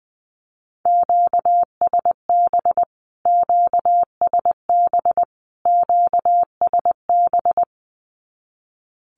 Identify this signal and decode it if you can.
Morse code, heard as a keyed tone, about 20 wpm, 700 Hz. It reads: QSB QSB QSB
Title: Q-Codes Repeated 3x 20wpm